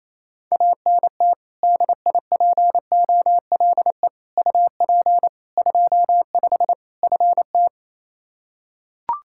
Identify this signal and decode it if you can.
Morse code, heard as a keyed tone, about 27 wpm, 700 Hz. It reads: ANT DIPOLE UP 25 FT
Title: Practice QSO - Exchange 2 - Info - 28wpm